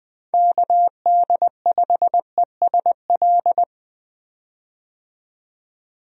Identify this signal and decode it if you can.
Morse code, heard as a keyed tone, about 20 wpm, 700 Hz. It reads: KD5ESL
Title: USA Call Signs - 6 Characters - 20wpm